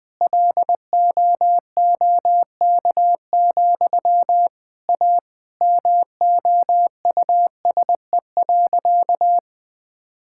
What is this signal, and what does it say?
Morse code, heard as a keyed tone, about 20 wpm, 700 Hz. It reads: LOOK, A MOUSE.
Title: Sentences from Top 1000 Words - Review Part 5 - 20wpm